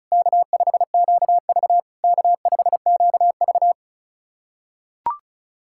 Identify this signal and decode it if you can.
Morse code, heard as a keyed tone, about 34 wpm, 700 Hz. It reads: K5QV K5QV
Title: SOTA QSO Practice 35wpm